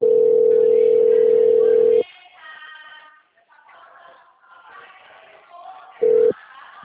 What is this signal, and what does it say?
Ringback tone (North America), 440 + 480 Hz.
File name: out- -2002- - - .21456